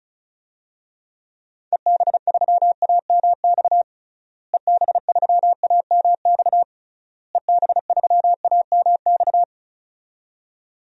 Morse code, heard as a keyed tone, about 36 wpm, 700 Hz. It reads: EB3AMX EB3AMX EB3AMX
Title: DXCC-100 Call Signs - 6 Characters - Repeated 3x - 35wpm